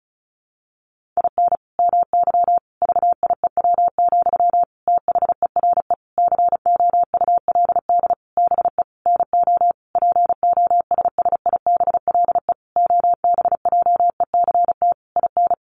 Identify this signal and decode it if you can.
Morse code, heard as a keyed tone, about 34 wpm, 700 Hz. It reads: IN MY VIEW, THERE COULD BE NO POSSIBLE OBJECT IN
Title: Sentences from Top 1500 Words 35wpm